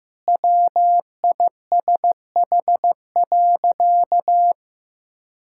Morse code, heard as a keyed tone, about 15 wpm, 700 Hz. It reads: WISH.
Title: Sentences from Top 1600 Words 15wpm